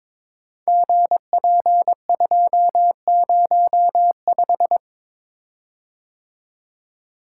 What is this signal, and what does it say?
Morse code, heard as a keyed tone, about 21 wpm, 700 Hz. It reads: GP205
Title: Code Groups - 5 Letter:Numbers - 22wpm